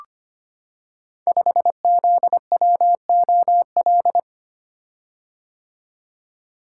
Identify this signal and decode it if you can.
Morse code, heard as a keyed tone, about 25 wpm, 700 Hz. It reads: 5ZWOL
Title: Code Groups - 5 Letter:Numbers - 25wpm